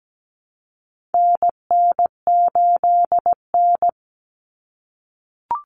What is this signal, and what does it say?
Morse code, heard as a keyed tone, about 17 wpm, 700 Hz. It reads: NN8N
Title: Common Cut Numbers - 4-Digit 17wpm